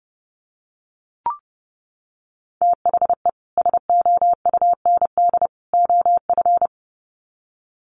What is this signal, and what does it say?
Morse code, heard as a keyed tone, about 30 wpm, 700 Hz. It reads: THE SOUND OF
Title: Sets of 3 Words 30wpm